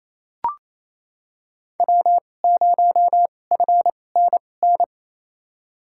Morse code, heard as a keyed tone, about 28 wpm, spaced as for 20 wpm, 700 Hz. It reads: W0FNN
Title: USA Call Signs - 5 Characters - 2x Character Spacing - 28wpm